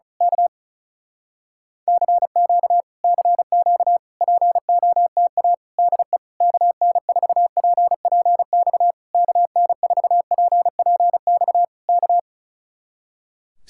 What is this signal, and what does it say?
Morse code, heard as a keyed tone, about 33 wpm, 700 Hz. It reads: K CQ CQ POTA DE KN4PPX KN4PPX K
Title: Speed-Racing Common QSO Phrases - 35wpm